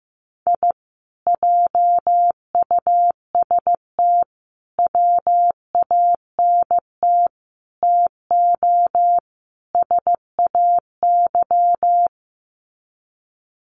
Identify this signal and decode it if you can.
Morse code, heard as a keyed tone, about 15 wpm, 700 Hz. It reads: I JUST WANT TO SAY
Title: Sets of 5 Words Encore 15wpm